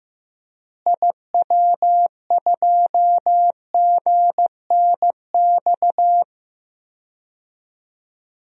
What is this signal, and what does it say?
Morse code, heard as a keyed tone, about 15 wpm, 700 Hz. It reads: IW2GNX
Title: DXCC-100 Call Signs - 6 Characters - Repeated 3x - 15wpm